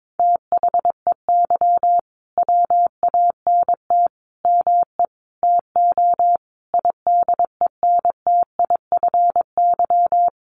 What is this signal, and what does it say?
Morse code, heard as a keyed tone, about 22 wpm, 700 Hz. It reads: THEY WANT ME TO IDENTIFY
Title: Sentences from Top 900 Words 22wpm